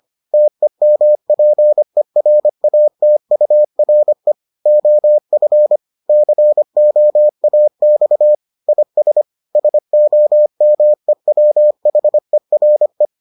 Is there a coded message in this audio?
Yes: Morse code from 0.3 to 13.1 s, heard as a keyed tone, about 25 wpm, 600 Hz. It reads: TEMPERATURE OF COAX IS SOMEWHERE